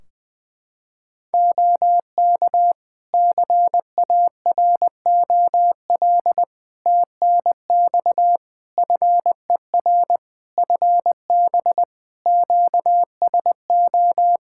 Morse code, heard as a keyed tone, about 19 wpm, 700 Hz. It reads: OK CAROL TNX FER FB QSO
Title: Practice QSO - Exchange 4 - Ending - 20wpm